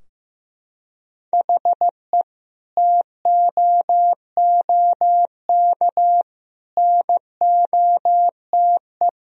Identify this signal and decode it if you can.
Morse code, heard as a keyed tone, about 15 wpm, 700 Hz. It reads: HE TOOK NOTE
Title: Sentences from Top 1000 Words - Review Part 2 - 15wpm